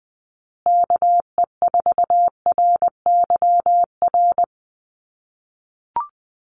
Morse code, heard as a keyed tone, about 20 wpm, 700 Hz. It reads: KE4RYR
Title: USA Call Signs Repeated 3x 20wpm